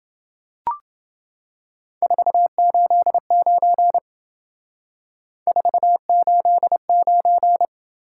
Morse code, heard as a keyed tone, about 30 wpm, 700 Hz. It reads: 489 489